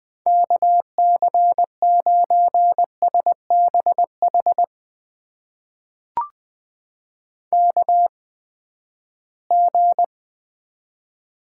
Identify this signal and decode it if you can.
Morse code, heard as a keyed tone, about 20 wpm, spaced as for 14 wpm, 700 Hz. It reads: KC9SBH KG
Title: USA Call Signs - 8x Character Spacing - 20wpm